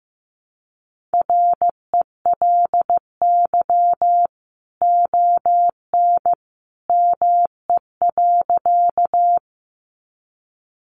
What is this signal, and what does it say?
Morse code, heard as a keyed tone, about 15 wpm, 700 Hz. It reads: RELY ON ME.
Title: Sentences from Top 1200 Words 15wpm